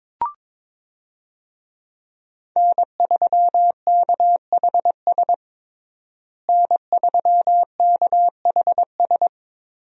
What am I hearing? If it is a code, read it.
Morse code, heard as a keyed tone, about 22 wpm, 700 Hz. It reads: N3KHS N3KHS
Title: USA Call Signs - 5 Characters - Repeated 3x - 22wpm